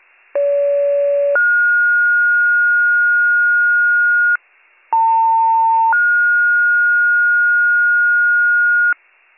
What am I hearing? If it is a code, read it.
Two-tone page, heard as a one-second tone, then a three-second tone, about 570 then 1430 Hz, about 900 then 1430 Hz, with noise. It is heard from 0.3 to 8.9 s.